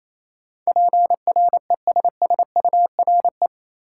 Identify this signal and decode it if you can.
Morse code, heard as a keyed tone, about 28 wpm, 700 Hz. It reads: PRESSURE
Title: Sentences from Top 1000 Words - Review Part 6 - New Word Drill - 28wpm